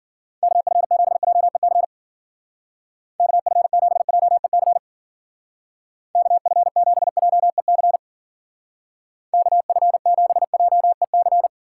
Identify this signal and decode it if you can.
Morse code, heard as a keyed tone, about 53 wpm, 700 Hz. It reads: KF7JEC KF7JEC KF7JEC KF7JEC